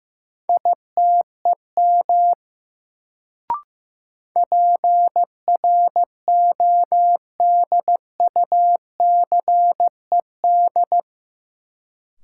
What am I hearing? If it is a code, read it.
Morse code, heard as a keyed tone, about 15 wpm, 700 Hz. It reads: ITEM PRODUCED
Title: Sentences from Top 1000 Words - Review Part 6 - New Word Drill - 15wpm